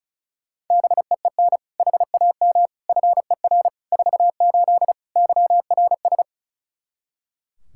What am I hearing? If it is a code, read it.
Morse code, heard as a keyed tone, about 36 wpm, 700 Hz. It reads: BEEN HAM FER 48 YRS
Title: Practice QSO - Exchange 3 - More Info - 35wpm